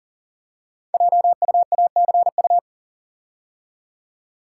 Morse code, heard as a keyed tone, about 40 wpm, 690 Hz. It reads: JUACU
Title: Code Groups - 5 Letters - 40wpm